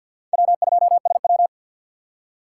Morse code, heard as a keyed tone, about 50 wpm, 690 Hz. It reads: W2SW
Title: USA Call Signs - 4 Characters - 6x Character Spacing - 50wpm